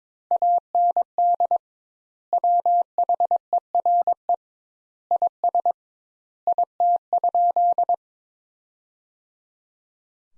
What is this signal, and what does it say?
Morse code, heard as a keyed tone, about 22 wpm, 700 Hz. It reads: AND WHERE IS IT?
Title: Sentences Top 200 Words - 22wpm